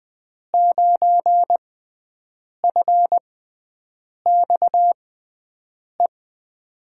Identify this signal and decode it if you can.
Morse code, heard as a keyed tone, about 21 wpm, spaced as for 7 wpm, 700 Hz. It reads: 9FXE
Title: USA Call Signs - 6 Characters - 6x Character Spacing - 20wpm